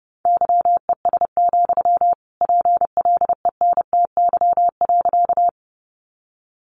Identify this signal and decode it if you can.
Morse code, heard as a keyed tone, about 30 wpm, 700 Hz. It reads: YES, PLENTY.
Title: Sentences from Top 1300 Words 30wpm